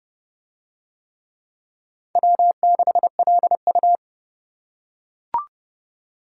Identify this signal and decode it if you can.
Morse code, heard as a keyed tone, about 31 wpm, 700 Hz. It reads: W6LU